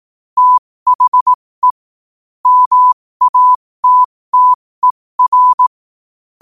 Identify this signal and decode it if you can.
Morse code, heard as a keyed tone, about 18 wpm, spaced as for 15 wpm, 1000 Hz. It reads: THE MATTER
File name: qotdfiles / 15wpm